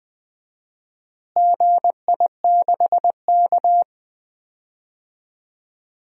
Morse code, heard as a keyed tone, about 20 wpm, 700 Hz. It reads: GI6K
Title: DXCC-100 Call Signs - 4 Characters - Repeated 2x - 20wpm